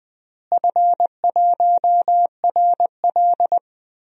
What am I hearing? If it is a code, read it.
Morse code, heard as a keyed tone, about 20 wpm, 700 Hz. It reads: F1RL
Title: DXCC-100 Call Signs - 4 Characters - Repeated 2x - 20wpm